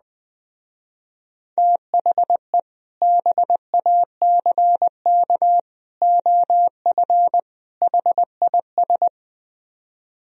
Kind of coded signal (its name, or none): Morse code